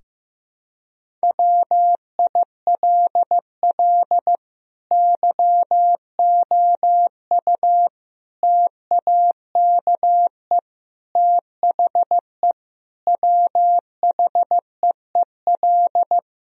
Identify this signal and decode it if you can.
Morse code, heard as a keyed tone, about 14 wpm, 700 Hz. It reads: WILL YOU TAKE THE WHEEL
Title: Sentences from Top 1000 Words - Review Part 3 - 15wpm